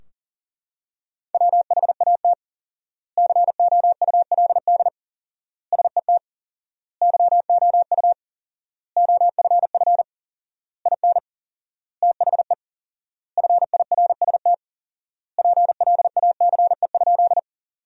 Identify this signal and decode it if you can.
Morse code, heard as a keyed tone, about 40 wpm, 700 Hz. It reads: WHAT COULD SET YOU OFF IN THE FIRST PLACE?
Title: 4x Word-Spacing - Sentences from Top 200 Words - 40wpm